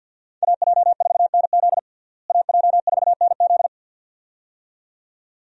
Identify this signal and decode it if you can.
Morse code, heard as a keyed tone, about 48 wpm, 700 Hz. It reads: AJ4NZ AJ4NZ
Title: USA Call Signs - 5 Characters - Repeated 2x - 50wpm